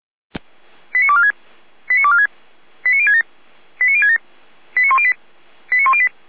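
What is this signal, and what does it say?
Five-tone selective call (ZVEI), heard as quick runs of five tones, tones of about 70 ms. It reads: 89246 89246 89067 89067 89108 89108
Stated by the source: Five-tone ZVEI selective call 89560, received on an SDR.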